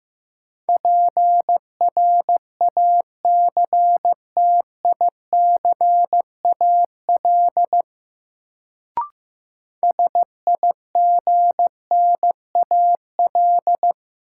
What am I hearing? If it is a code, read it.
Morse code, heard as a keyed tone, about 16 wpm, 700 Hz. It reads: PRACTICAL SIGNAL
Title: Sentences from Top 1400 Words - New Word Drill Alt - 15wpm